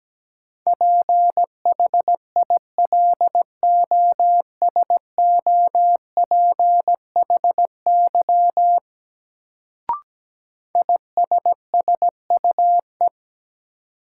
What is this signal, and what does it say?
Morse code, heard as a keyed tone, about 17 wpm, 700 Hz. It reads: PHILOSOPHY ISSUE
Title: Sentences from Top 900 Words - New Word Drill - 17wpm